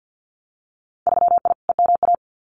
Morse code, heard as a keyed tone, about 50 wpm, 700 Hz. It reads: 3I EMA
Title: Winter Field Day Exchanges Encore 50wpm